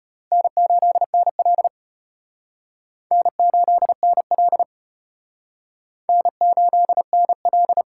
Morse code, heard as a keyed tone, about 32 wpm, 700 Hz. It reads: N8NL N8NL N8NL